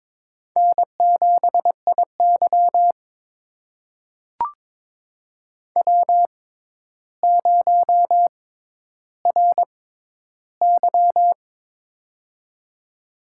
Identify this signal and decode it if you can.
Morse code, heard as a keyed tone, about 23 wpm, spaced as for 11 wpm, 700 Hz. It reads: N7IY W0RY
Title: USA Call Signs - 4 Characters - 6x Character Spacing - 22wpm